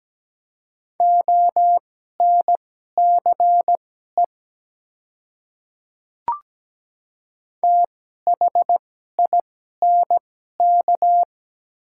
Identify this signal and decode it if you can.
Morse code, heard as a keyed tone, about 17 wpm, spaced as for 12 wpm, 690 Hz. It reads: ONCE THINK
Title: 2x Character Spacing - Top 300 Words - 17wpm